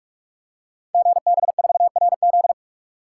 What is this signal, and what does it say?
Morse code, heard as a keyed tone, about 44 wpm, 700 Hz. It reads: GB4RZ
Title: Code Groups - 5 Letter:Numbers - 45wpm